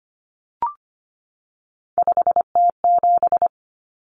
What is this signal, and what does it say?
Morse code, heard as a keyed tone, about 25 wpm, 700 Hz. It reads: 5T7
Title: Common Cut Numbers - 3-Digit 25wpm